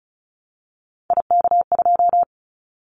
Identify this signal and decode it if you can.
Morse code, heard as a keyed tone, about 36 wpm, 700 Hz. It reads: IK2
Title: Callsign Prefixes for Top 50 DXCC - 35wpm